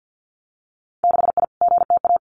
Morse code, heard as a keyed tone, about 50 wpm, 700 Hz. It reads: 6I GTA
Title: Winter Field Day Exchanges Encore 50wpm